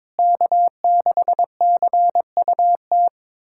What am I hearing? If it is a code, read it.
Morse code, heard as a keyed tone, about 22 wpm, 700 Hz. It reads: K6CUT
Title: USA Callsign DE Callsign - 5 Characters - 22wpm